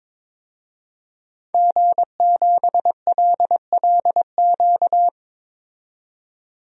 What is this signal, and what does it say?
Morse code, heard as a keyed tone, about 22 wpm, 700 Hz. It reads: G7LLQ
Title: DXCC-100 Call Signs - 5 Characters - 22wpm